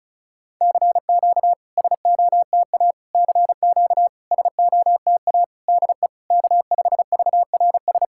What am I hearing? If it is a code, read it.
Morse code, heard as a keyed tone, about 36 wpm, 690 Hz. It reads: CQ SOTA CQ SOTA DE K5VRS
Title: SOTA QSO Practice 35wpm